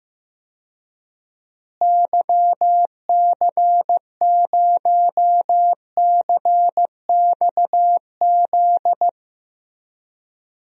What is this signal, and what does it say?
Morse code, heard as a keyed tone, about 15 wpm, 700 Hz. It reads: YC0CXZ
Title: DXCC-100 Call Signs - 6 Characters - Repeated 2x - 15wpm